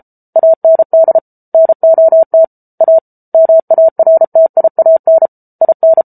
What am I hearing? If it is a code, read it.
Morse code, heard as a keyed tone, about 34 wpm, 650 Hz. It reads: AND NOT A MARTIAN IN